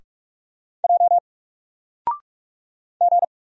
Morse code, heard as a keyed tone, about 44 wpm, 700 Hz. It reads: J G
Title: Single Character (Letter:Number:Punctuation:Prosign) Alt 45wpm